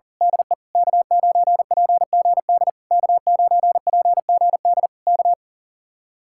Morse code, heard as a keyed tone, about 38 wpm, 700 Hz. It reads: DE K9PGD K9PGD K
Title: Speed-Racing Common QSO Phrases Encore - 40wpm